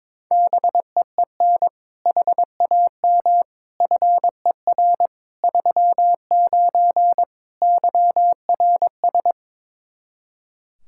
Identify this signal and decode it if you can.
Morse code, heard as a keyed tone, about 22 wpm, 700 Hz. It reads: BEEN HAM FER 39 YRS
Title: Practice QSO - Exchange 3 - More Info - 22wpm